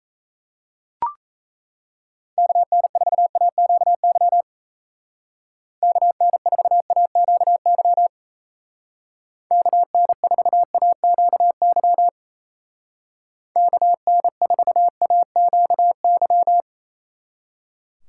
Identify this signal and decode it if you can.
Morse code, heard as a keyed tone, about 34 wpm, 700 Hz. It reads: KN4AQY KN4AQY KN4AQY KN4AQY
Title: Speed Racing - USA Call Signs 28wpm